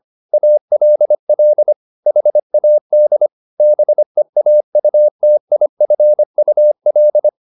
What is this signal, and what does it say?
Morse code, heard as a keyed tone, about 25 wpm, 600 Hz. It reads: ALL HAD BEAUTIFUL